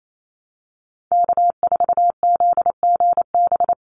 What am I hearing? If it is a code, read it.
Morse code, heard as a keyed tone, about 28 wpm, 700 Hz. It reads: K4ZGB
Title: Call Signs Repeated 2x - 28wpm